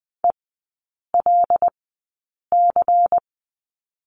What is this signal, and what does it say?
Morse code, heard as a keyed tone, about 20 wpm, spaced as for 8 wpm, 700 Hz. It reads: ELC
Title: 093 Lesson 16e - Letter c - Three Characters 20wpm